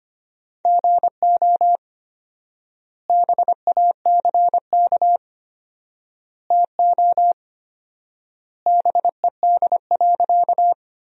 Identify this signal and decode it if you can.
Morse code, heard as a keyed tone, about 25 wpm, 700 Hz. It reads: GO BACK TO BED.
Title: 4x Word-Spacing - Sentences from Top 500 Words - 25wpm